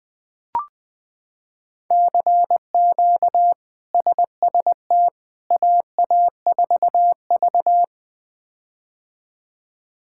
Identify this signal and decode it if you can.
Morse code, heard as a keyed tone, about 20 wpm, 700 Hz. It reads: CQ SST AA4V
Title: K1USN SST QSO Practice - 20wpm